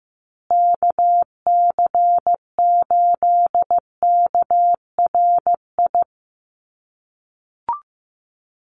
Morse code, heard as a keyed tone, about 15 wpm, 700 Hz. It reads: KC8KRI
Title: USA Call Signs Repeated 3x 15wpm